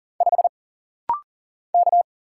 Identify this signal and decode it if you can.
Morse code, heard as a keyed tone, about 40 wpm, 700 Hz. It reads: 5 K
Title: Single Letter-Number Alt - Rapid-Fire 40wpm